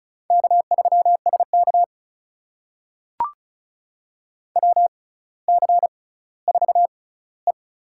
Morse code, heard as a keyed tone, about 36 wpm, spaced as for 18 wpm, 700 Hz. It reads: K3SK WC4E
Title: USA Call Signs - 4 Characters - 6x Character Spacing - 35wpm